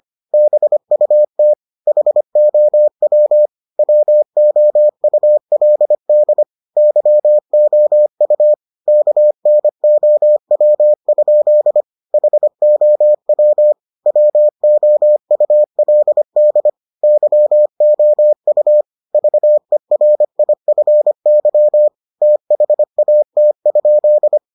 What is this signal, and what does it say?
Morse code, heard as a keyed tone, about 25 wpm, 600 Hz. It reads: BUT HOW WOULD YOU KNOW? HOW WOULD YOU VERIFY THAT?